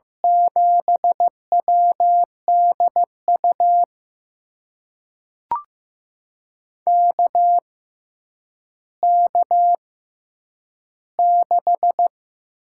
Morse code, heard as a keyed tone, about 15 wpm, spaced as for 9 wpm, 700 Hz. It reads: 7WDU KK6
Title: USA Call Signs - 6 Characters - 6x Character Spacing - 15wpm